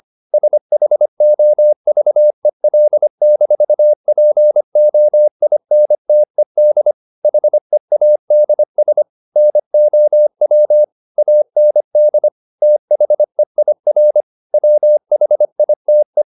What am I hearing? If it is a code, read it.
Morse code, heard as a keyed tone, about 25 wpm, 600 Hz. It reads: SHOVEL-POINTED HEADS NOW AND THEIR WHITE